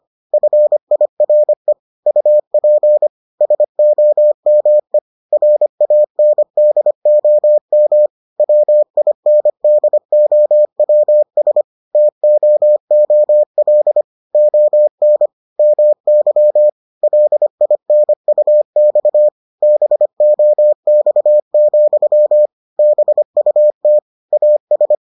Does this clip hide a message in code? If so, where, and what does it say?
Yes: Morse code from 0.3 to 24.9 s, heard as a keyed tone, about 25 wpm, 600 Hz. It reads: FIRE UP SOME RANDOM WINDOWS TOOL ON MY LINUX BOX, BUT AS